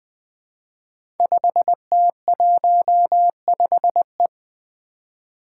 Morse code, heard as a keyed tone, about 20 wpm, 700 Hz. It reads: 5T15E
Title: Five Challenging Letter-Number Combinations 20wpm